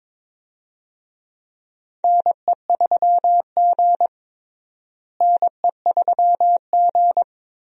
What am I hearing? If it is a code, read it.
Morse code, heard as a keyed tone, about 21 wpm, 700 Hz. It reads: NE3G NE3G
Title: USA Call Signs - 4 Characters - Repeated 3x - 22wpm